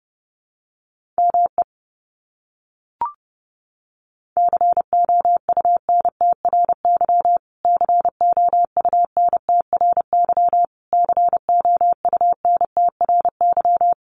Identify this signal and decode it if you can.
Morse code, heard as a keyed tone, about 30 wpm, 700 Hz. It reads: ME COUNTRY COUNTRY COUNTRY
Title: Top 300 Words Repeated 30wpm